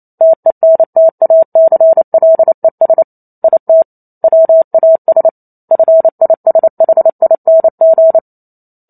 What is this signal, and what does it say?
Morse code, heard as a keyed tone, about 28 wpm, 650 Hz. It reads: TENTACLES IT WAS FISHING